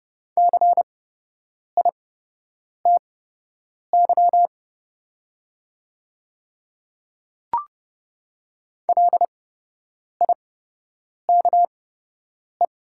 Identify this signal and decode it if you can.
Morse code, heard as a keyed tone, about 31 wpm, spaced as for 8 wpm, 700 Hz. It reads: CITY LIKE
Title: 8x Character Spacing - Top 500 Words - 30wpm